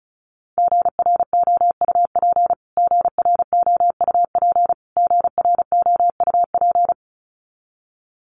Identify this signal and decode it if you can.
Morse code, heard as a keyed tone, about 34 wpm, 700 Hz. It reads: GROUP GROUP GROUP
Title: Top 300 Words Repeated 35wpm